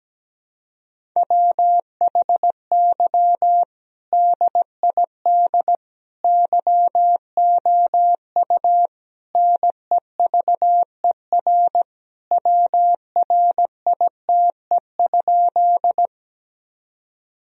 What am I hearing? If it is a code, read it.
Morse code, heard as a keyed tone, about 17 wpm, 700 Hz. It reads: WHY DID YOU NEVER WRITE?
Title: Sentences from Top 1000 Words - Review Part 1 - 17wpm